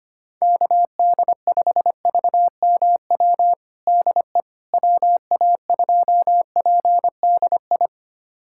Morse code, heard as a keyed tone, about 25 wpm, 700 Hz. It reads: KD5VMW DE WA2PDI
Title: USA Callsign DE Callsign - 6 Characters - 25wpm